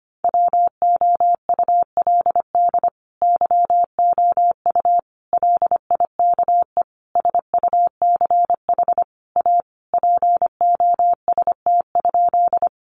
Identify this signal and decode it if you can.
Morse code, heard as a keyed tone, about 25 wpm, 700 Hz. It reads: WOULD YOU LIKE SUCH A POST?
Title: Sentences from Top 1000 Words 25wpm